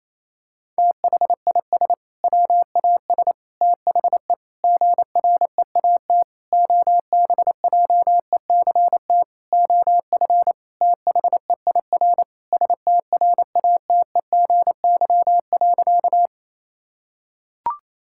Morse code, heard as a keyed tone, about 28 wpm, 700 Hz. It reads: THIS WAS THE GREAT OBJECT OF THEIR STRATEGY.
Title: Sentences from Top 1000 Words - Review Part 5 - 28wpm